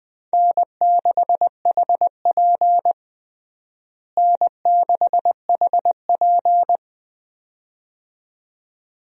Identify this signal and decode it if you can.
Morse code, heard as a keyed tone, about 20 wpm, 700 Hz. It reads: N6HP N6HP
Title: USA Call Signs - 4 Characters - Repeated 2x - 20wpm